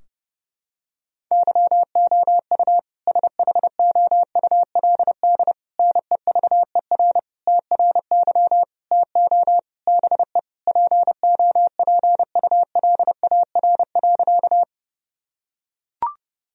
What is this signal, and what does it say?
Morse code, heard as a keyed tone, about 30 wpm, 700 Hz. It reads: YOU SHOULD NEVER TRY TO BE POPULAR.
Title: Sentences from Top 1000 Words - Review Part 3 - 30wpm